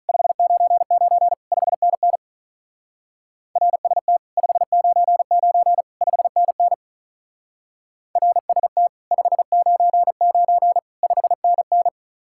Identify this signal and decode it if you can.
Morse code, heard as a keyed tone, about 42 wpm, 700 Hz. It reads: 599 5NN RST 599 5NN RST 599 5NN
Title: Speed-Racing Common QSO Phrases - 35wpm